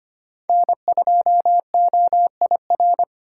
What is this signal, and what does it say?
Morse code, heard as a keyed tone, about 25 wpm, 700 Hz. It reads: N2OIR
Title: USA Call Signs - 5 Characters - 4x Character Spacing - 25wpm